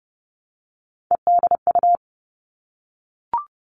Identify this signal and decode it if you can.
Morse code, heard as a keyed tone, about 30 wpm, 700 Hz. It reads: EDU
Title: Three Challenging Letter-Number Combinations 30wpm